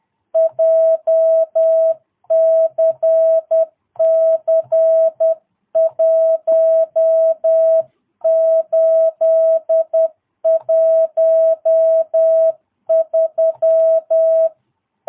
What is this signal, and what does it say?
Morse code, heard as a keyed tone, about 10 wpm, 660 Hz. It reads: JCC 1813
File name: jh8cbh / 2022cw